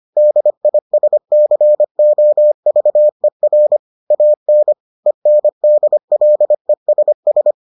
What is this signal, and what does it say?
Morse code, heard as a keyed tone, about 25 wpm, 600 Hz. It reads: DISCOVER AN ENDLESS